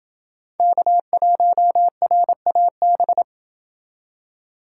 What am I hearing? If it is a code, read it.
Morse code, heard as a keyed tone, about 29 wpm, 700 Hz. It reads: K1RAB